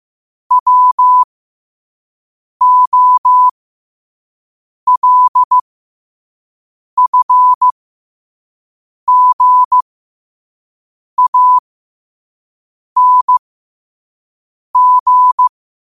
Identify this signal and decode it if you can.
Morse code, heard as a keyed tone, about 15 wpm, spaced as for 5 wpm, 1000 Hz. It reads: WOLFGANG
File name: qotdfiles / 5wpm